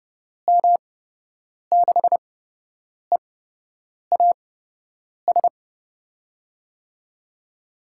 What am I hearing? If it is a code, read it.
Morse code, heard as a keyed tone, about 29 wpm, spaced as for 8 wpm, 700 Hz. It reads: M6EAS
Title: USA Call Signs - 6 Characters - 8x Character Spacing - 30wpm